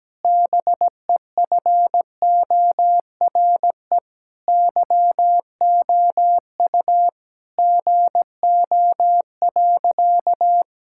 Morse code, heard as a keyed tone, about 17 wpm, 700 Hz. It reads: BEFORE YOU GO.
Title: Sentences from Top 1000 Words - Review Part 6 - 17wpm